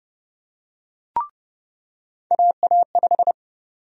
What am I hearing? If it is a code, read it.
Morse code, heard as a keyed tone, about 30 wpm, 700 Hz. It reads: AA5
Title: USA Call Sign Prefixes - 30wpm